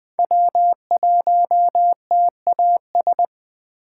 Morse code, heard as a keyed tone, about 24 wpm, 700 Hz. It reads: W1TAS
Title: Speed Racing - USA 1x3 Call Signs 20wpm